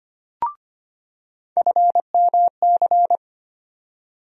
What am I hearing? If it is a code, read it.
Morse code, heard as a keyed tone, about 25 wpm, 700 Hz. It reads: FMC
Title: USA Call Sign Postfixes - 3 Letters 25wpm